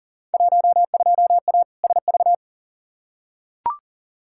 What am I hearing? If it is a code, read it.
Morse code, heard as a keyed tone, about 39 wpm, 700 Hz. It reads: 12A SV
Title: Field Day Exchanges Encore - 40wpm